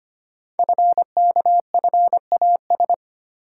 Morse code, heard as a keyed tone, about 25 wpm, 700 Hz. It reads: FKFAS
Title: Code Groups - 5 Letter:Numbers - 25wpm